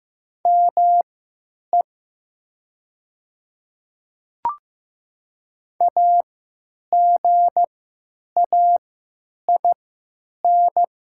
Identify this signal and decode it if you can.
Morse code, heard as a keyed tone, about 15 wpm, spaced as for 9 wpm, 700 Hz. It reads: ME AGAIN
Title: 3x Character Spacing - Top 200 Words - 15wpm